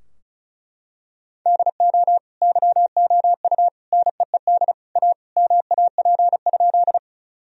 Morse code, heard as a keyed tone, about 36 wpm, 700 Hz. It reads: DO YOU NEED A MAP?
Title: Sentences from Top 1000 Words - Review Part 2 - 35wpm